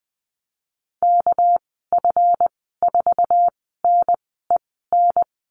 Morse code, heard as a keyed tone, about 20 wpm, spaced as for 15 wpm, 700 Hz. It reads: KF4NEN
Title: 200 Lesson 32g - Letter k - Call Signs 20wpm